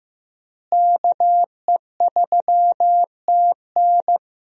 Morse code, heard as a keyed tone, about 15 wpm, 700 Hz. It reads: KE3TN
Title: USA Call Signs - 5 Characters - Repeated 3x - 15wpm